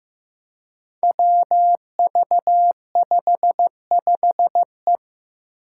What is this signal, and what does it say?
Morse code, heard as a keyed tone, about 15 wpm, 700 Hz. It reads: WV55E
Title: Five Challenging Letter-Number Combinations 15wpm